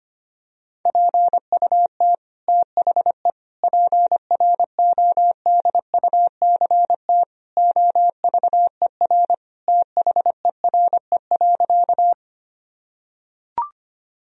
Morse code, heard as a keyed tone, about 25 wpm, 700 Hz. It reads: PUT THE PRODUCT OVER THERE.
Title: Sentences from Top 1000 Words - Review Part 2 - 25wpm